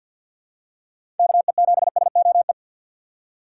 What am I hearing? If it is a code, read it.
Morse code, heard as a keyed tone, about 52 wpm, 700 Hz. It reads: KE7SOE
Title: USA Call Signs - 6 Characters - 4x Character Spacing - 50wpm